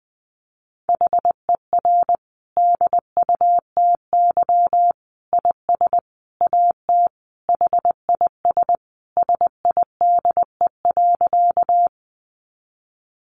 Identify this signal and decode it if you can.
Morse code, heard as a keyed tone, about 20 wpm, 700 Hz. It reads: HER DUTY IS AT HIS SIDE.
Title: Sentences from Top 1500 Words 20wpm